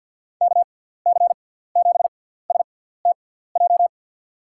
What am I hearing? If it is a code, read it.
Morse code, heard as a keyed tone, about 48 wpm, spaced as for 17 wpm, 700 Hz. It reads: KC7STJ
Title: USA Call Signs - 6 Characters - 6x Character Spacing - 50wpm